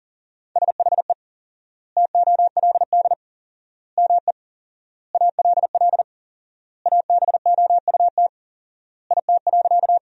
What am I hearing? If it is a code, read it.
Morse code, heard as a keyed tone, about 37 wpm, 700 Hz. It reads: SHE TOLD ME ALL ABOUT IT.
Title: 4x Word-Spacing - Sentences from Top 400 Words - 40wpm